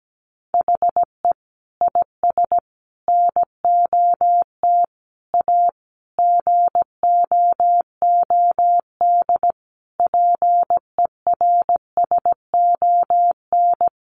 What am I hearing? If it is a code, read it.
Morse code, heard as a keyed tone, about 17 wpm, 700 Hz. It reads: HE IS NOT A GOOD PERSON
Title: Sentences from Top 1400 Words 17wpm